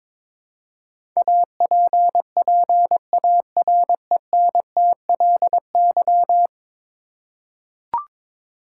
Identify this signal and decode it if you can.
Morse code, heard as a keyed tone, about 22 wpm, 700 Hz. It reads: APPARENTLY
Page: morse-code-ninja-practice-22wpm